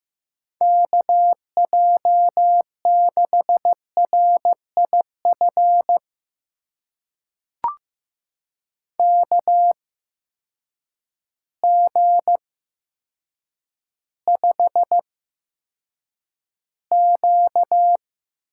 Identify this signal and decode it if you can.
Morse code, heard as a keyed tone, about 15 wpm, spaced as for 7 wpm, 700 Hz. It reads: KJ6RIF KG5Q
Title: USA Call Signs - 8x Character Spacing - 15wpm